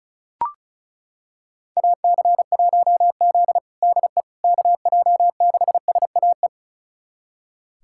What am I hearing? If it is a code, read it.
Morse code, heard as a keyed tone, about 36 wpm, 700 Hz. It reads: AC1Z DE KJ6SAE
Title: USA Callsign DE Callsign 35wpm